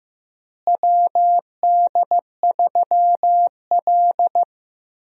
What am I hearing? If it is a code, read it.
Morse code, heard as a keyed tone, about 15 wpm, 700 Hz. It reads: WD3L
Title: USA Call Signs - 4 Characters - Repeated 2x - 15wpm